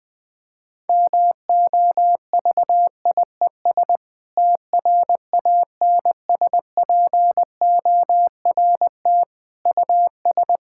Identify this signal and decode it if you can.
Morse code, heard as a keyed tone, about 20 wpm, 700 Hz. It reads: MOVIES TRANSPORT US
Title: Difficult Sentences with Hints - 20wpm